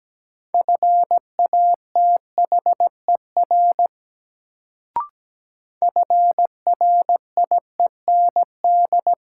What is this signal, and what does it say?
Morse code, heard as a keyed tone, about 17 wpm, 700 Hz. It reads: FATHER FRIEND
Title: Sentences from Top 300 Words - New Word Drill - 17wpm